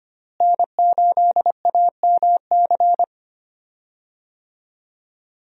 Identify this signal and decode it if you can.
Morse code, heard as a keyed tone, about 25 wpm, 700 Hz. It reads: N8AMC
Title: USA Call Signs - 5 Characters - Repeated 3x - 25wpm